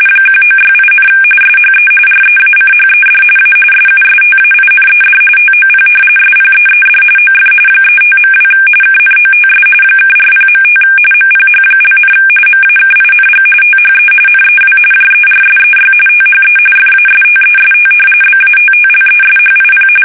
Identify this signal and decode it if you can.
RTTY, about 45.45 baud. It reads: 1968, SHE REACHED A SUSTAINED TOP SPEED OF 35.2 KNOTS (40.5 MPH), MAKING THE NEW JERSEY THE FASTEST BATTLESHIP
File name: fsk850 / battleshipnj